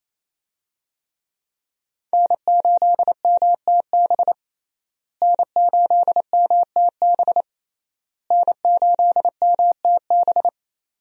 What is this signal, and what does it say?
Morse code, heard as a keyed tone, about 28 wpm, 700 Hz. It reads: N8MTB N8MTB N8MTB
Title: USA Call Signs - 5 Characters - Repeated 3x - 28wpm